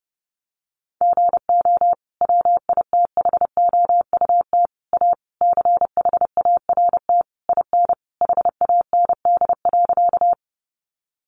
Morse code, heard as a keyed tone, about 30 wpm, 700 Hz. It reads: GO WITHOUT A CHART IN HAND.
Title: Sentences from Top 1300 Words 30wpm